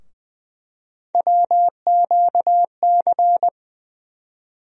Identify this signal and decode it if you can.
Morse code, heard as a keyed tone, about 20 wpm, 700 Hz. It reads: WQC
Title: USA Call Sign Postfixes - 3 Letters 20wpm